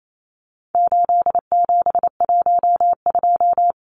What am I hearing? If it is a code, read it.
Morse code, heard as a keyed tone, about 28 wpm, 700 Hz. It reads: 8712
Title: Numbers - 4-Digit 28wpm